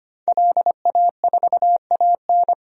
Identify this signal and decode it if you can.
Morse code, heard as a keyed tone, about 25 wpm, 700 Hz. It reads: LA4AN
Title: DXCC-100 Call Signs - 5 Characters - Repeated 2x - 25wpm